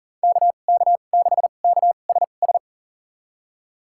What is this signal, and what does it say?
Morse code, heard as a keyed tone, about 37 wpm, spaced as for 28 wpm, 700 Hz. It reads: KK6KSS
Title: USA Call Signs - 6 Characters - 2x Character Spacing - 40wpm